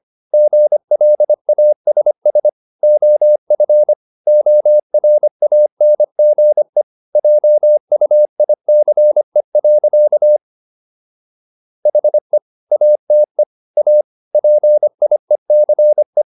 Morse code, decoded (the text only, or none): GLASS OF ORANGE JUICE. HE ATE A PIECE